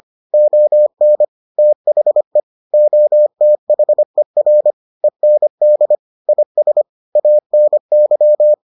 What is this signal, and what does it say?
Morse code, heard as a keyed tone, about 25 wpm, 600 Hz. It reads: ON THE OTHER END IS ANY